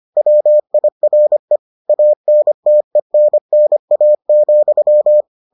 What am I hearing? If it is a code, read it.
Morse code, heard as a keyed tone, about 25 wpm, 600 Hz. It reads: WIRE ANTENNA,